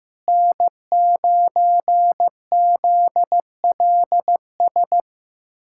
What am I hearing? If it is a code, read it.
Morse code, heard as a keyed tone, about 15 wpm, 700 Hz. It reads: N9ZLS
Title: USA Callsign DE Callsign - 5 Characters - 15wpm